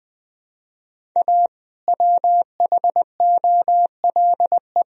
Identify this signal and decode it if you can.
Morse code, heard as a keyed tone, about 21 wpm, 700 Hz. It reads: A WHOLE
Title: Sentences from Top 1000 Words - Review Part 1 - 20wpm